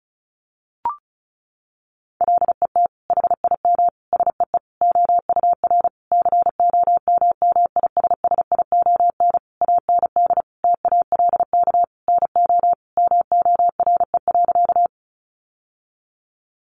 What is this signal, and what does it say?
Morse code, heard as a keyed tone, about 34 wpm, 700 Hz. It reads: LET HIM SEE OUR COMMISSION AND TALK NO MORE.
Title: Sentences from Top 1400 Words 35wpm